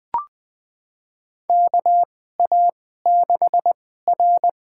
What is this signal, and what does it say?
Morse code, heard as a keyed tone, about 20 wpm, spaced as for 15 wpm, 700 Hz. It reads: KA6R
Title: USA Call Signs - 4 Characters - 2x Character Spacing - 20wpm